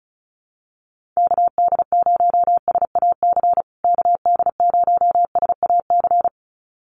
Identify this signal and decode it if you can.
Morse code, heard as a keyed tone, about 34 wpm, 700 Hz. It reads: KD0SAC KD0SAC
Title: USA Call Signs Repeated 2x 35wpm